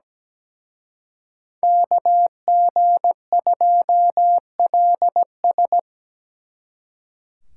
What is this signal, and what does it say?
Morse code, heard as a keyed tone, about 20 wpm, 700 Hz. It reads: KG2LS
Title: Speed Racing - USA 2x2 Call Signs 17wpm